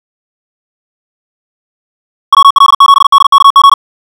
Morse code, heard as a keyed tone, about 56 wpm, 1110 Hz. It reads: WL7DKG
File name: ja7uhv / SampleHST2